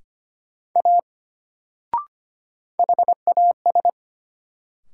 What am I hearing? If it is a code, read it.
Morse code, heard as a keyed tone, about 25 wpm, 700 Hz. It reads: A HAS
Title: Sentences from Top 100 Words - New Word Drill - 25wpm